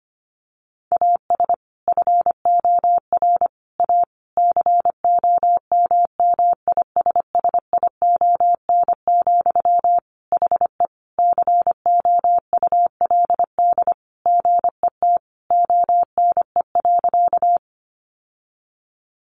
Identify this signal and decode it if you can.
Morse code, heard as a keyed tone, about 25 wpm, 700 Hz. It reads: AS FOR A COMMISSION, HE COULD GET ONE.
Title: Sentences from Top 1400 Words 25wpm